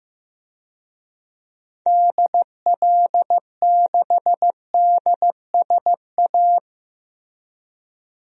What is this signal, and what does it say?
Morse code, heard as a keyed tone, about 15 wpm, 700 Hz. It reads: DL6DSA
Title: DXCC-100 Call Signs - 6 Characters - Repeated 2x - 15wpm